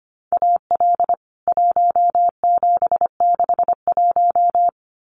Morse code, heard as a keyed tone, about 25 wpm, 700 Hz. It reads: AL 1761
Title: CWT Exchange 25wpm